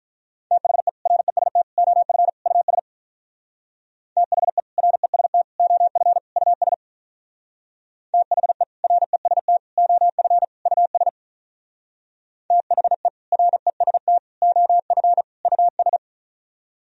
Morse code, heard as a keyed tone, about 43 wpm, 700 Hz. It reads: THE REST OF US THE REST OF US THE REST OF US THE REST OF US
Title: Speed-Racing - Sets of 4 Words 35wpm